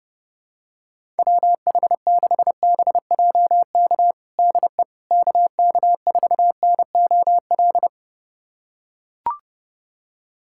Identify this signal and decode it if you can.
Morse code, heard as a keyed tone, about 30 wpm, 700 Hz. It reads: WH6BJK DE KK4NOL
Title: USA Callsign DE Callsign 30wpm